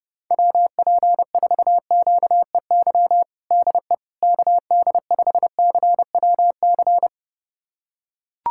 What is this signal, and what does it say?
Morse code, heard as a keyed tone, about 30 wpm, 700 Hz. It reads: WP4QEY DE KD5CWC
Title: USA Callsign DE Callsign 30wpm